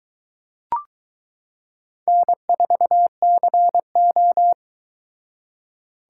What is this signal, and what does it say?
Morse code, heard as a keyed tone, about 23 wpm, 700 Hz. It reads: N4CO